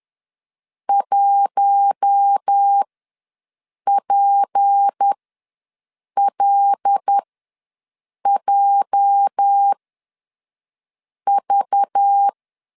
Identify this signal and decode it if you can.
Morse code, heard as a keyed tone, about 10 wpm, spaced as for 5 wpm, 800 Hz. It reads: 1PLJV
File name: LEZIONI / 8 YJ45 / 8 YJ ws 10wpm 4043